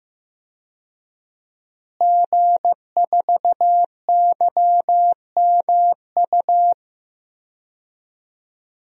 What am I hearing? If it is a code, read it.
Morse code, heard as a keyed tone, about 16 wpm, 700 Hz. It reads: G4YMU
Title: DXCC-100 Call Signs - 5 Characters - Repeated 2x - 15wpm